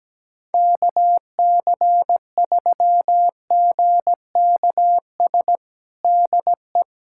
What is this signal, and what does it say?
Morse code, heard as a keyed tone, about 17 wpm, 700 Hz. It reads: KC3GKS DE
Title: USA Callsign DE Callsign - 6 Characters - 17wpm